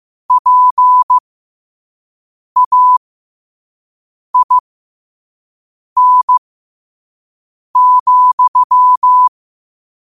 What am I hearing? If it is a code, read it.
Morse code, heard as a keyed tone, about 15 wpm, spaced as for 5 wpm, 1000 Hz. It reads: PAIN,
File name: qotdfiles / 5wpm